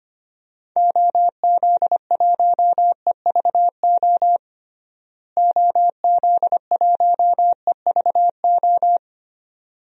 Morse code, heard as a keyed tone, about 25 wpm, 700 Hz. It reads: OZ1EVO OZ1EVO
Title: DXCC-100 Call Signs - 6 Characters - Repeated 2x - 25wpm